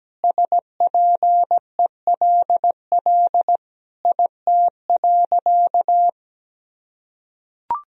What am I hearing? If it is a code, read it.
Morse code, heard as a keyed tone, about 17 wpm, 700 Hz. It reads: SPELL IT.
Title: Sentences from Top 1000 Words - Review Part 1 - 17wpm